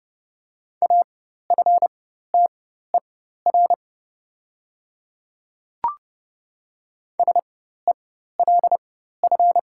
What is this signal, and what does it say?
Morse code, heard as a keyed tone, about 30 wpm, spaced as for 14 wpm, 700 Hz. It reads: AFTER SELF
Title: 4x Character Spacing - Top 500 Words - 30wpm